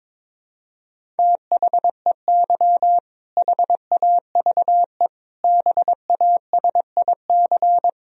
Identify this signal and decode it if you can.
Morse code, heard as a keyed tone, about 22 wpm, 700 Hz. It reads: THEY HAVE BASIC
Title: Sentences from Top 1000 Words - Review Part 3 - 22wpm